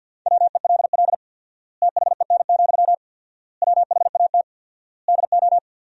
Morse code, heard as a keyed tone, about 51 wpm, 700 Hz. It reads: WELL THEN, WHAT DO
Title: 4x Word-Spacing - Sentences from Top 200 Words - 50wpm